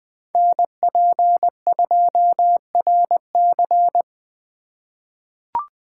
Morse code, heard as a keyed tone, about 20 wpm, 700 Hz. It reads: NP2RC
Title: USA Call Signs - 8x Character Spacing - 20wpm